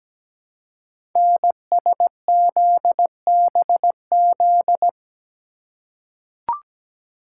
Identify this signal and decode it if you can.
Morse code, heard as a keyed tone, about 17 wpm, 700 Hz. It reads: NSZBZ
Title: Code Groups - 5 Letters - 17wpm